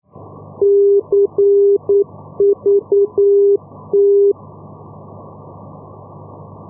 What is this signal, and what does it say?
Morse code, heard as a keyed tone, about 9 wpm, 400 Hz. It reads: CVT
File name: europe / e-448 ndb madrid-cuatro vientos cvt-285 np madrid 2016-05